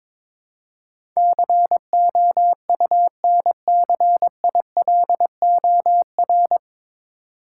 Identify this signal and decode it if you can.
Morse code, heard as a keyed tone, about 22 wpm, 700 Hz. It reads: COUNCILOR
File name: 9 Letter Words - First 3 Letters Given - 22wpm